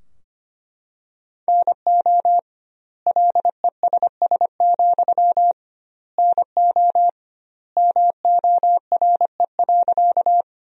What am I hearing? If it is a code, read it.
Morse code, heard as a keyed tone, about 25 wpm, 700 Hz. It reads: NO LESS, NO MORE.
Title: 2x Word-Spacing - Sentences from Top 400 Words - 25wpm